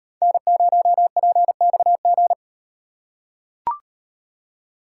Morse code, heard as a keyed tone, about 36 wpm, 700 Hz. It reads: N0PXG